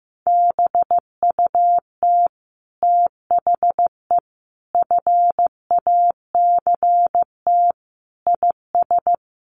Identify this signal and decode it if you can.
Morse code, heard as a keyed tone, about 15 wpm, 700 Hz. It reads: BUT THE FACT IS
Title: Sets of 5 Words Encore2 15wpm